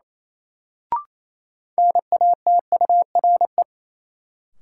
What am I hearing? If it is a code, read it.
Morse code, heard as a keyed tone, about 28 wpm, 700 Hz. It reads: NATURE
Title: Sentences from Top 600 Words - New Word Drill - 28wpm